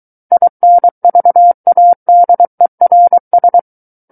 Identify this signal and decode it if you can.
Morse code, heard as a keyed tone, about 23 wpm, 700 Hz. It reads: INVADERS
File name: Book One / 14-Chapter 14 - 23 WPM